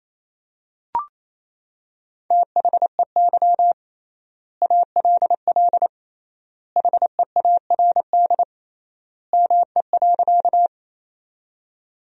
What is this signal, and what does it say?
Morse code, heard as a keyed tone, about 27 wpm, 700 Hz. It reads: THEY ALL HEARD ME.
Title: 3x Word-Spacing - Sentences from Top 400 Words - 28wpm